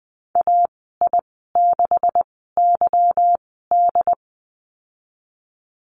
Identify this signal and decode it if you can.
Morse code, heard as a keyed tone, about 19 wpm, spaced as for 14 wpm, 700 Hz. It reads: AI6YD
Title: 147 Lesson 24g - Letter y - Call Signs 20wpm